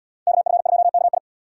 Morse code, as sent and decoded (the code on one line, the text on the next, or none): -.- ..-. ..--- -.- ..
KF2KI